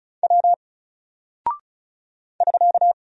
Morse code, heard as a keyed tone, about 36 wpm, 700 Hz. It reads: W <SK>
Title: Single Character (Letter:Number:Punctuation:Prosign) 35wpm